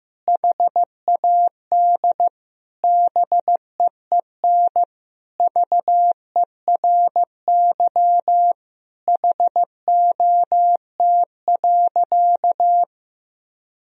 Morse code, heard as a keyed tone, about 15 wpm, 700 Hz. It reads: HAD BEEN VERY HOT.
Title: Sentences from Top 1000 Words - Review Part 3 - 15wpm